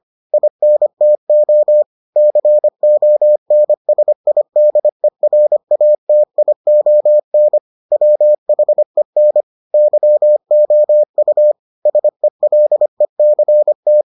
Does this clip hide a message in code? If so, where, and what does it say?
Yes: Morse code from 0.3 to 14.0 s, heard as a keyed tone, about 25 wpm, 600 Hz. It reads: INTO CONSIDERATION WHEN YOU SELECT